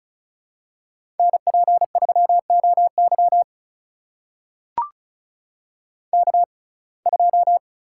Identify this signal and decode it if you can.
Morse code, heard as a keyed tone, about 36 wpm, spaced as for 25 wpm, 700 Hz. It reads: NP3OY K2
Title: USA Call Signs - 6x Character Spacing - 35wpm